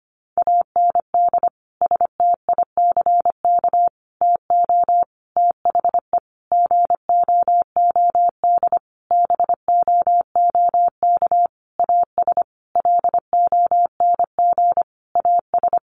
Morse code, heard as a keyed tone, about 25 wpm, 700 Hz. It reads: AND STICK TO THE GOOD BOOK AS LONG AS
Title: Sentences from Top 1200 Words 25wpm